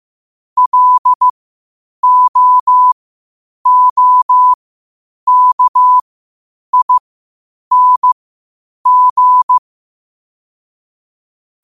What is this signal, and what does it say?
Morse code, heard as a keyed tone, about 15 wpm, spaced as for 8 wpm, 1000 Hz. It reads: LOOKING
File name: qotdfiles / 8wpm